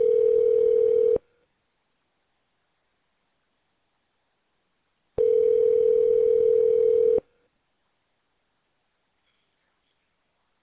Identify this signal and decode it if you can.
Ringback tone (North America), 440 + 480 Hz.